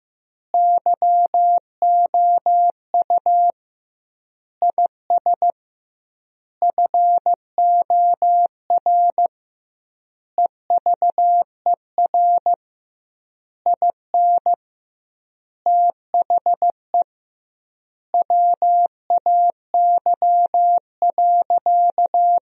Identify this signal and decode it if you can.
Morse code, heard as a keyed tone, about 15 wpm, 700 Hz. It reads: YOU IS FOR EVER IN THE WAY.
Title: 2x Word-Spacing - Sentences from Top 400 Words - 15wpm